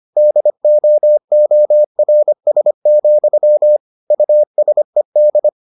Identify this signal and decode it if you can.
Morse code, heard as a keyed tone, about 25 wpm, 600 Hz. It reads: DOORS, USED